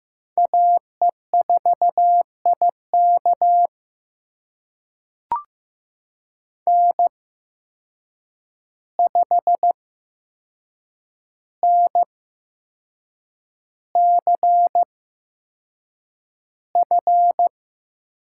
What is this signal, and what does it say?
Morse code, heard as a keyed tone, about 15 wpm, spaced as for 6 wpm, 700 Hz. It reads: AE4IK N5NCF
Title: USA Call Signs - 5 Characters - 8x Character Spacing - 15wpm